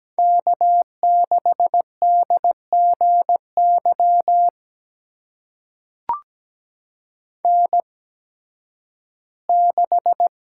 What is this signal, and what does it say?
Morse code, heard as a keyed tone, about 17 wpm, spaced as for 11 wpm, 700 Hz. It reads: K6DGY N6
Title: USA Call Signs - 5 Characters - 8x Character Spacing - 17wpm